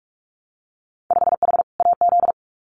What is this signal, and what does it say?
Morse code, heard as a keyed tone, about 46 wpm, 690 Hz. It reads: 5H AZ
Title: Winter Field Day Exchanges Encore 45wpm